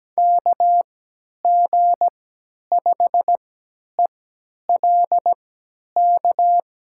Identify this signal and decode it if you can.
Morse code, heard as a keyed tone, about 17 wpm, spaced as for 10 wpm, 700 Hz. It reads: KG5ELK
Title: USA Call Signs - 3x Character Spacing - 17wpm